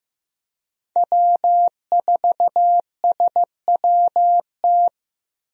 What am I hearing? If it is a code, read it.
Morse code, heard as a keyed tone, about 16 wpm, 700 Hz. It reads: W4SWT
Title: USA Call Signs - 5 Characters - Repeated 2x - 15wpm